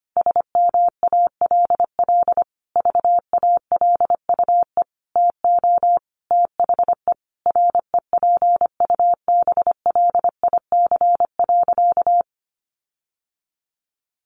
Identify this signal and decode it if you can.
Morse code, heard as a keyed tone, about 25 wpm, 700 Hz. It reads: SMALL VALUE TO THE REPUBLIC.
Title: Sentences from Top 1500 Words 25wpm